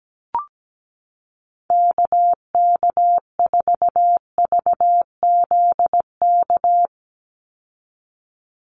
Morse code, heard as a keyed tone, about 17 wpm, 700 Hz. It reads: KK4VZK
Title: USA Call Signs 17wpm